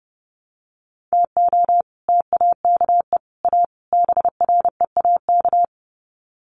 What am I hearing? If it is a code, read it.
Morse code, heard as a keyed tone, about 30 wpm, 700 Hz. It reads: TO TAKE A BREAK
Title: Sets of 4 Words 30wpm